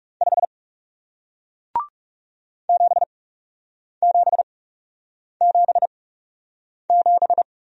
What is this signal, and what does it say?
Morse code, heard as a keyed tone, about 40 wpm, spaced as for 8 wpm, 700 Hz. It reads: 5 7777